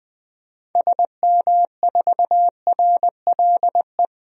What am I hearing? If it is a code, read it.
Morse code, heard as a keyed tone, about 20 wpm, 700 Hz. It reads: SM4RLE
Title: DXCC-100 Call Signs - 6 Characters - 20wpm